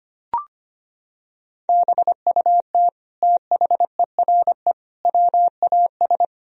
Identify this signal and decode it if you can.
Morse code, heard as a keyed tone, about 25 wpm, 700 Hz. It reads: BUT THERE WAS
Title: Sentences from Top 1000 Words - Review Part 2 - 25wpm